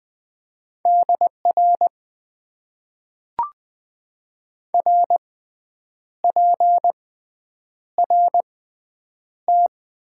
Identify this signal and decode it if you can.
Morse code, heard as a keyed tone, about 20 wpm, spaced as for 8 wpm, 700 Hz. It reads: DR RPRT
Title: Top 90 QSO Elements - 6x Character Spacing - 20wpm